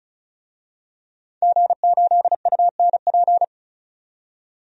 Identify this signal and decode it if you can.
Morse code, heard as a keyed tone, about 34 wpm, 700 Hz. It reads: G8UNP
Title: DXCC-100 Call Signs - 5 Characters - 35wpm